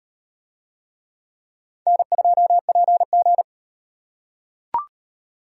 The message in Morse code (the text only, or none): N2PG